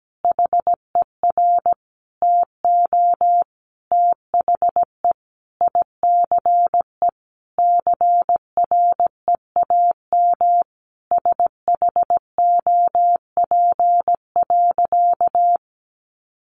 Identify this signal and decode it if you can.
Morse code, heard as a keyed tone, about 17 wpm, 700 Hz. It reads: HER TO THE ICE CREAM SHOP.
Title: Easy Phrases 17wpm